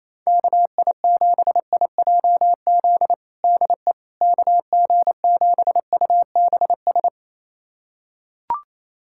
Morse code, heard as a keyed tone, about 28 wpm, 700 Hz. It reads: KI7IJZ DE KG7UBS
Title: USA Callsign DE Callsign 28wpm